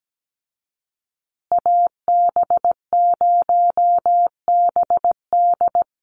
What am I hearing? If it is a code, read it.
Morse code, heard as a keyed tone, about 17 wpm, 700 Hz. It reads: AB0BD
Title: USA Call Signs Repeated 2x 17wpm